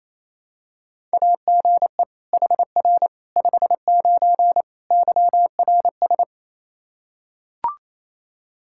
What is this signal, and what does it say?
Morse code, heard as a keyed tone, about 28 wpm, 700 Hz. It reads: AGE HR 59 YRS
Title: Practice QSO - Exchange 3 - More Info - 28wpm